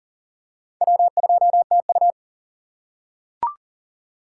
Morse code, heard as a keyed tone, about 41 wpm, 700 Hz. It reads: W2TU
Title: USA Call Signs - 4 Characters - 3x Character Spacing - 40wpm